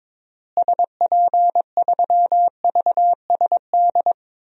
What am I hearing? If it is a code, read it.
Morse code, heard as a keyed tone, about 22 wpm, 700 Hz. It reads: SP3VSD
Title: DXCC-100 Call Signs - 6 Characters - Repeated 3x - 22wpm